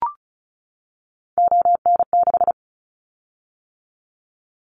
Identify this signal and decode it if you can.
Morse code, heard as a keyed tone, about 34 wpm, 700 Hz. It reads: ON6
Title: Callsign Prefixes for Top 100 DXCC - 35wpm